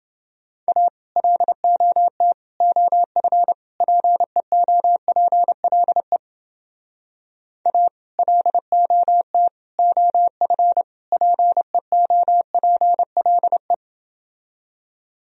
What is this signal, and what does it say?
Morse code, heard as a keyed tone, about 30 wpm, 700 Hz. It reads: A LOT OF PEOPLE A LOT OF PEOPLE